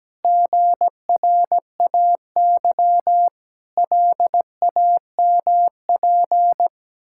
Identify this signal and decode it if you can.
Morse code, heard as a keyed tone, about 17 wpm, 700 Hz. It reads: GRAY LAMP
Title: Pangram Challenge v2 - 17wpm